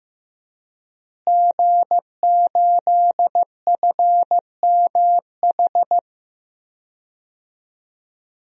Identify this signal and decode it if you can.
Morse code, heard as a keyed tone, about 15 wpm, 700 Hz. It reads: G8FMH
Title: DXCC-100 Call Signs - 5 Characters - Repeated 3x - 15wpm